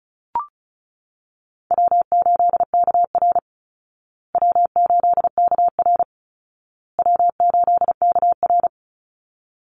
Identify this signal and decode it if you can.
Morse code, heard as a keyed tone, about 34 wpm, 700 Hz. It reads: W8KR W8KR W8KR
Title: Easy Call Signs Repeated 3x - 35wpm